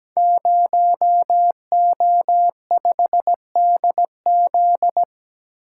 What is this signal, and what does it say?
Morse code, heard as a keyed tone, about 17 wpm, 700 Hz. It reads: 0O5DZ
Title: Code Groups - 5 Letter:Numbers - 17wpm